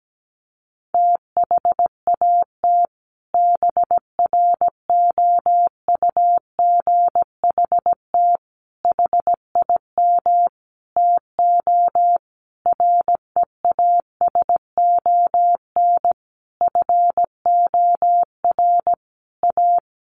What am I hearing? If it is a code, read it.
Morse code, heard as a keyed tone, about 17 wpm, 700 Hz. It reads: THAT BROUGHT HIM TO REASON FOR A
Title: Sentences from Top 700 Words 17wpm